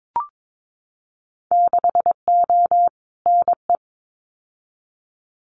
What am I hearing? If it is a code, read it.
Morse code, heard as a keyed tone, about 22 wpm, 700 Hz. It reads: 6O NE
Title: Winter Field Day Exchanges Encore 22wpm